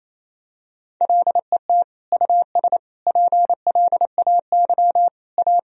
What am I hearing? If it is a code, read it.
Morse code, heard as a keyed tone, about 28 wpm, 700 Hz. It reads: LET US PLAY A
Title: Sentences from Top 1000 Words - Review Part 3 - 28wpm